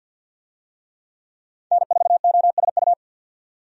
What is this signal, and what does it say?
Morse code, heard as a keyed tone, about 50 wpm, 700 Hz. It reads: N4OSU
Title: USA Call Signs - 5 Characters - 50wpm